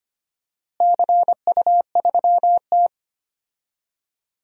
Morse code, heard as a keyed tone, about 25 wpm, 700 Hz. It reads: CU3T
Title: DXCC-100 Call Signs - 4 Characters - Repeated 2x - 25wpm